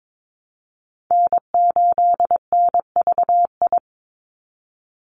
Morse code, heard as a keyed tone, about 22 wpm, 700 Hz. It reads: N8NVI
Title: USA Call Signs 22wpm